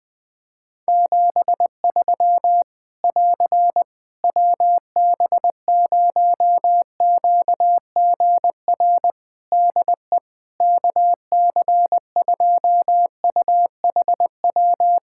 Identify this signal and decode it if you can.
Morse code, heard as a keyed tone, about 21 wpm, 700 Hz. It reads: 73 <AR> WB0QGR DE KC2UHW
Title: Practice QSO - Exchange 4 - Ending - 20wpm